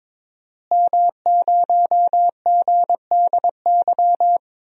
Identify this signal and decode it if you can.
Morse code, heard as a keyed tone, about 22 wpm, 700 Hz. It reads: M0GDY
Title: DXCC-100 Call Signs - 5 Characters - Repeated 2x - 22wpm